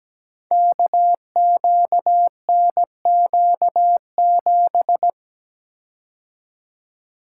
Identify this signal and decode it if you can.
Morse code, heard as a keyed tone, about 17 wpm, 700 Hz. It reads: KQNQ7
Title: Code Groups - 5 Letter:Numbers - 17wpm